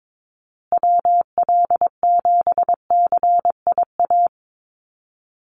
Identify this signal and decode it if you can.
Morse code, heard as a keyed tone, about 22 wpm, 700 Hz. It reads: WL7CIA
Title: USA Call Signs Repeated 2x 22wpm